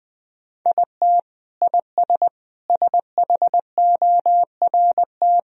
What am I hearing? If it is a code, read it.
Morse code, heard as a keyed tone, about 20 wpm, 700 Hz. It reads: IT IS SHORT
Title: Sentences from Top 1000 Words - Review Part 2 - 20wpm